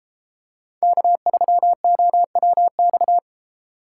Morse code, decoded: K3OWX

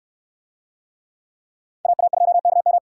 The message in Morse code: AA2KM

